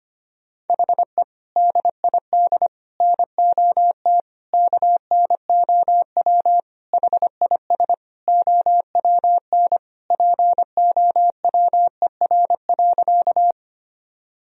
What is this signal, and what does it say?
Morse code, heard as a keyed tone, about 25 wpm, 700 Hz. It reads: HE DID NOT KNOW HIS OWN POWER.
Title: Sentences from Top 1000 Words - Review Part 2 - 25wpm